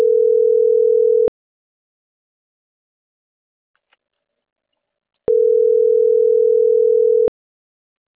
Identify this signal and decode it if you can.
Ringback tone (North America), 440 + 480 Hz.